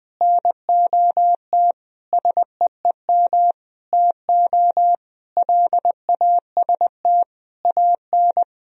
Morse code, heard as a keyed tone, about 20 wpm, 700 Hz. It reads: NOT SEEM TO LAST AN
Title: Sentences from Top 1000 Words - Review Part 1 - 20wpm